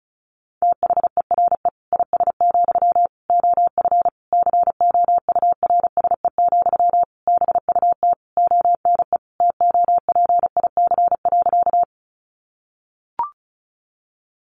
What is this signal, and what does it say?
Morse code, heard as a keyed tone, about 34 wpm, 700 Hz. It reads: THERE IS, OF COURSE, BUT ONE TOPIC.
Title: Sentences from Top 1200 Words 35wpm